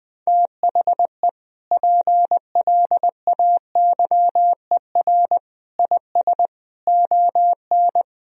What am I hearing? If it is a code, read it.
Morse code, heard as a keyed tone, about 21 wpm, 700 Hz. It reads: THE PLAYER IS ON
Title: Sentences from Top 1000 Words - Review Part 4 - 20wpm